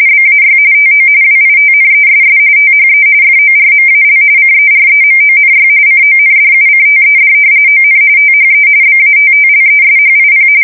RTTY, about 45.45 baud. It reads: BAY TO WITNESS THE FORMAL CEREMONY CONDUCTED ON THE BATTLESHIP
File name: fsk170 / usnavy1